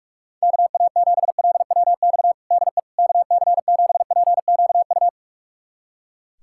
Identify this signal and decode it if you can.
Morse code, heard as a keyed tone, about 41 wpm, 700 Hz. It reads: KA7LWX DE KC7PQU
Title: USA Callsign DE Callsign - 6 Characters - 45wpm